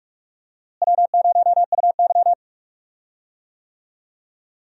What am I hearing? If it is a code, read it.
Morse code, heard as a keyed tone, about 41 wpm, 700 Hz. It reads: W0UY
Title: USA Call Signs - 4 Characters - Repeated 3x - 45wpm